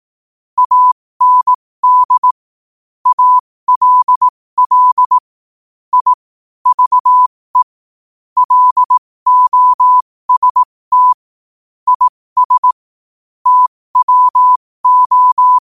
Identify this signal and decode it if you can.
Morse code, heard as a keyed tone, about 18 wpm, spaced as for 15 wpm, 1000 Hz. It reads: AND ALL I VE LOST IS TWO
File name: qotdfiles / 15wpm